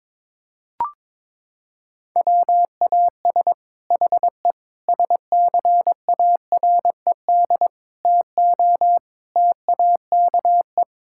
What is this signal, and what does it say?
Morse code, heard as a keyed tone, about 22 wpm, 700 Hz. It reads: WAS HE SCARED TO TAKE
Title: Sentences from Top 1000 Words - Review Part 6 - 22wpm